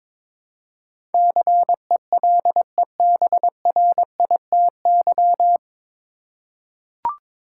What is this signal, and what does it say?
Morse code, heard as a keyed tone, about 21 wpm, 700 Hz. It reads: CELEBRITY
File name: 9 Letter Words - First 3 Letters Given - 22wpm